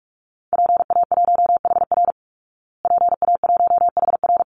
Morse code, heard as a keyed tone, about 44 wpm, 700 Hz. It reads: PA1HR PA1HR
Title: Call Signs Repeated 2x - 45wpm